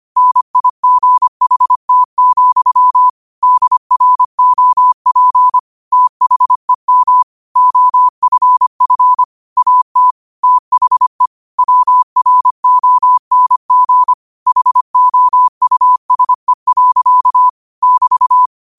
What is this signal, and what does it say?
Morse code, heard as a keyed tone, about 25 wpm, 1000 Hz. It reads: NIGHT, DROP THEM OFF AT THE WRONG HOUSE. <BT>
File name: qotdfiles / 25wpm